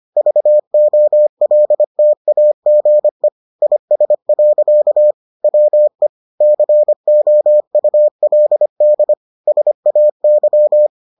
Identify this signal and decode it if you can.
Morse code, heard as a keyed tone, about 25 wpm, 600 Hz. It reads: VOLTAGE IS. WE COULD SAY